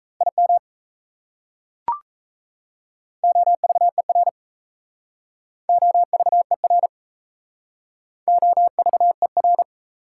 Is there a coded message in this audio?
Yes: Morse code from 0.2 to 9.6 s, heard as a keyed tone, about 36 wpm, 700 Hz. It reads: IM OVER OVER OVER